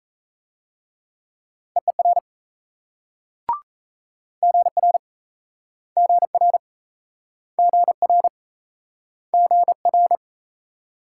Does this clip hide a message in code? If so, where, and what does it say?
Yes: Morse code from 1.8 to 10.2 s, heard as a keyed tone, about 38 wpm, 700 Hz. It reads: EER GR GR GR GR